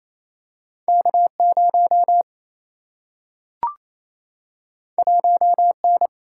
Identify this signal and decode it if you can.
Morse code, heard as a keyed tone, about 28 wpm, 700 Hz. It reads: K 0 1 N
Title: Single-Letter Single-Number 28wpm